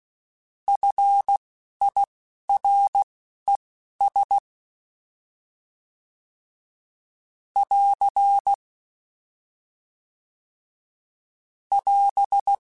Morse code, heard as a keyed tone, about 16 wpm, spaced as for 11 wpm, 800 Hz. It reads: FIRES <AR> <AS>
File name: F Runs / F Runs mp3s / F Runs @10